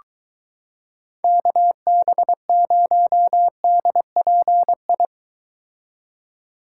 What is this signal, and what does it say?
Morse code, heard as a keyed tone, about 23 wpm, 700 Hz. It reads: KB0DPI